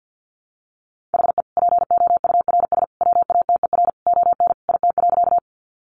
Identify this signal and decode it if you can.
Morse code, heard as a keyed tone, about 50 wpm, 700 Hz. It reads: HE POURS WATER ON IT.
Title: Additional Easy Phrases 50wpm